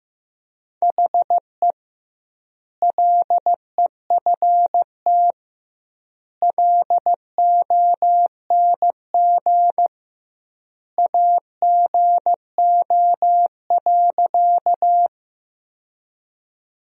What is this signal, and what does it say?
Morse code, heard as a keyed tone, about 16 wpm, 700 Hz. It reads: HE LEFT LONG AGO.
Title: 2x Word-Spacing - Sentences from Top 500 Words - 15wpm